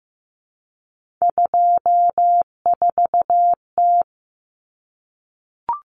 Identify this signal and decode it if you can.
Morse code, heard as a keyed tone, about 15 wpm, 700 Hz. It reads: 24T
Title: Common Cut Numbers - 3-Digit 15wpm